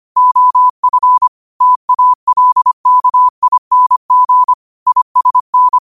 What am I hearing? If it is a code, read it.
Morse code, heard as a keyed tone, about 25 wpm, 1000 Hz. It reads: OF TALKING ISN
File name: qotdfiles / 25wpm